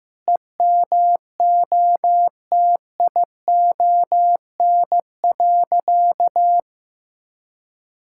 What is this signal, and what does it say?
Morse code, heard as a keyed tone, about 15 wpm, 700 Hz. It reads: EMOTION.
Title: Sentences from Top 1600 Words 15wpm